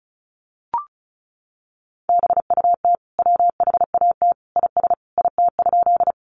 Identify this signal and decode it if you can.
Morse code, heard as a keyed tone, about 34 wpm, 700 Hz. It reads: BUT WHAT IS IT?
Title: Sentences from Top 100 Words 35wpm